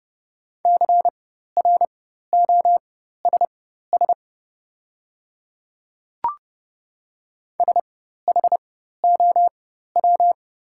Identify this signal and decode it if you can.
Morse code, heard as a keyed tone, about 31 wpm, spaced as for 14 wpm, 690 Hz. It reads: CROSS SHOW
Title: 4x Character Spacing - Top 500 Words - 30wpm